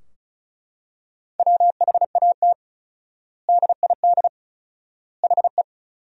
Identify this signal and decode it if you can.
Morse code, heard as a keyed tone, about 36 wpm, 700 Hz. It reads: WHAT DID HE
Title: 4x Word-Spacing - Sentences from Top 300 Words - 35wpm